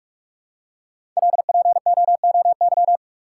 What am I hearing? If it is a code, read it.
Morse code, heard as a keyed tone, about 44 wpm, 700 Hz. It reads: LPOOY
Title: Code Groups - 5 Letters - 45wpm